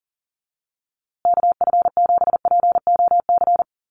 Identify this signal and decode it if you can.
Morse code, heard as a keyed tone, about 40 wpm, 700 Hz. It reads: KF7POC
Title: USA Call Signs Repeated 2x 40wpm